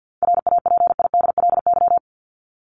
Morse code, heard as a keyed tone, about 50 wpm, 700 Hz. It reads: RAPIDLY
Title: LY Suffix - Top 100 Words with Base Words 50wpm